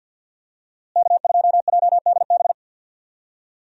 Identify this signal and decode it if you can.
Morse code, heard as a keyed tone, about 52 wpm, 700 Hz. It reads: K2JDB
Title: USA Call Signs - 5 Characters - Repeated 3x - 50wpm